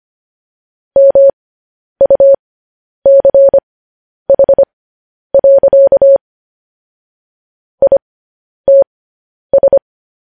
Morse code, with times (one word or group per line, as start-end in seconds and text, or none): 1.0-6.2 MUCH.
7.8-9.8 ITS